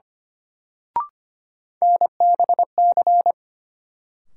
Morse code, heard as a keyed tone, about 25 wpm, 700 Hz. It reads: NBC
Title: Sets of 5 Words Encore - New Word Drill - 25wpm